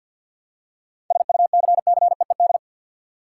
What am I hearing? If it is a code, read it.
Morse code, heard as a keyed tone, about 50 wpm, 700 Hz. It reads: SUCCEED
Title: Seven Letter Words - 50wpm